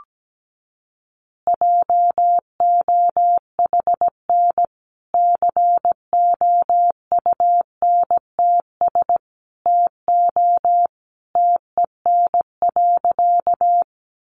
Morse code, heard as a keyed tone, about 17 wpm, 700 Hz. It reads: JOHN COUNTS TO TEN.
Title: Easy Phrases 17wpm